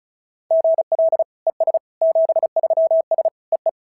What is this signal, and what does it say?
Morse code, heard as a keyed tone, about 36 wpm, 690 Hz. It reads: GL ES 73S EE
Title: SOTA QSO Practice 35wpm